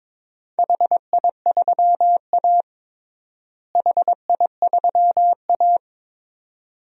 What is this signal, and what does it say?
Morse code, heard as a keyed tone, about 22 wpm, 700 Hz. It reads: HI3A HI3A
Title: DXCC-100 Call Signs - 4 Characters - Repeated 2x - 22wpm